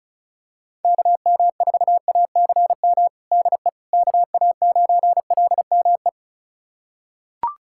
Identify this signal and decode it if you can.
Morse code, heard as a keyed tone, about 36 wpm, 700 Hz. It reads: KM4ACM DE KA9LME
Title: USA Callsign DE Callsign 35wpm